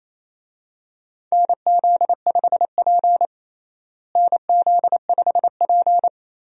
Morse code, heard as a keyed tone, about 28 wpm, 700 Hz. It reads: NZ5P NZ5P
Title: USA Call Signs - 4 Characters - Repeated 2x - 28wpm